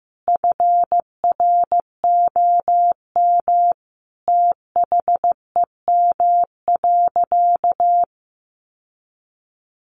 Morse code, heard as a keyed tone, about 15 wpm, 700 Hz. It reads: FROM THEM.
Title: Sentences from Top 900 Words 15wpm